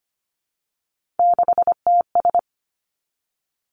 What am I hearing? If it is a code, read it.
Morse code, heard as a keyed tone, about 25 wpm, 700 Hz. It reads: 6TS
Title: Three Challenging Letter-Number Combinations 25wpm